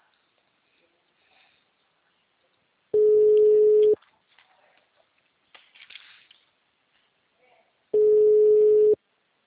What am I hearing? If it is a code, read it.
Ringback tone (Europe, 425 Hz).